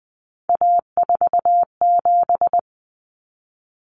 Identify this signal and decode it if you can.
Morse code, heard as a keyed tone, about 20 wpm, 700 Hz. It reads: A47
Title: Common Cut Numbers - 3-Digit 20wpm